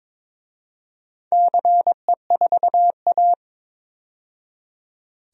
Morse code, heard as a keyed tone, about 22 wpm, 700 Hz. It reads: CE4A
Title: DXCC-100 Call Signs - 4 Characters - 22wpm